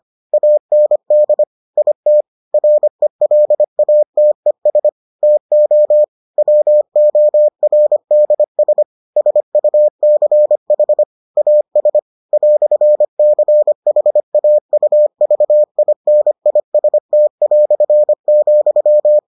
Morse code, heard as a keyed tone, about 25 wpm, 600 Hz. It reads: AND IT RELATES TO WORDS SUCH AS "CHAUVINIST",